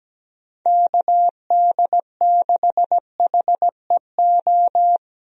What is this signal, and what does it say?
Morse code, heard as a keyed tone, about 17 wpm, 700 Hz. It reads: KD6HEO
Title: USA Call Signs - 6 Characters - 17wpm